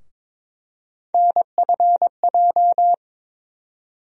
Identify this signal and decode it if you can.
Morse code, heard as a keyed tone, about 22 wpm, 700 Hz. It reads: NFJ
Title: USA Call Sign Postfixes - 3 Letters 22wpm